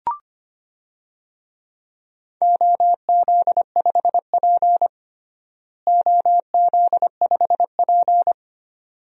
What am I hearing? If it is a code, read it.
Morse code, heard as a keyed tone, about 25 wpm, 700 Hz. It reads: OZ5P OZ5P
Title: DXCC-100 Call Signs - 4 Characters - Repeated 2x - 25wpm